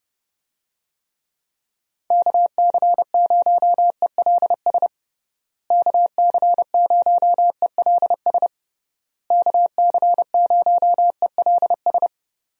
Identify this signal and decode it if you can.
Morse code, heard as a keyed tone, about 32 wpm, 700 Hz. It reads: KC0ELS KC0ELS KC0ELS
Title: USA Call Signs - 6 Characters - Repeated 3x - 30wpm